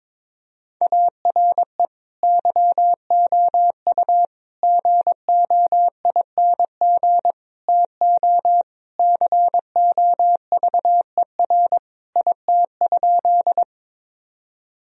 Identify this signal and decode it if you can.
Morse code, heard as a keyed tone, about 22 wpm, 700 Hz. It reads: ARE YOU GOING TO COVER IT?
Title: Sentences from Top 1000 Words - Review Part 1 - 22wpm